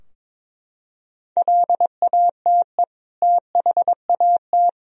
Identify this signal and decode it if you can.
Morse code, heard as a keyed tone, about 22 wpm, 700 Hz. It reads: LATE THAT
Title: Sentences from Top 1000 Words - Review Part 4 - 22wpm